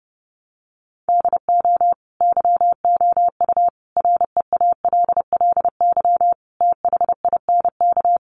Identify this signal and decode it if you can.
Morse code, heard as a keyed tone, about 30 wpm, 700 Hz. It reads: DO YOU REALLY THINK
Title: Sentences from Top 600 Words 30wpm